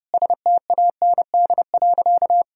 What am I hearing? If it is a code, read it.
Morse code, heard as a keyed tone, about 30 wpm, 700 Hz. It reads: STAND.
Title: Sentences from Top 1000 Words - Review Part 1 - 30wpm